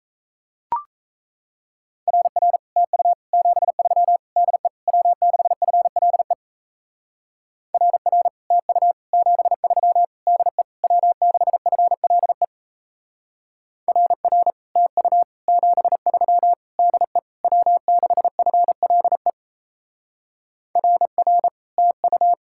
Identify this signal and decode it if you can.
Morse code, heard as a keyed tone, about 33 wpm, 700 Hz. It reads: RR TU 73 DE W6FLE RR TU 73 DE W6FLE RR TU 73 DE W6FLE RR TU
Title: Speed-Racing Common QSO Phrases - 28wpm